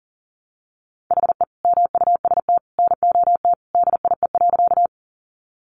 Morse code, heard as a keyed tone, about 40 wpm, 700 Hz. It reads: HE MUST NOT DIE.
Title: Sentences from Top 1300 Words 40wpm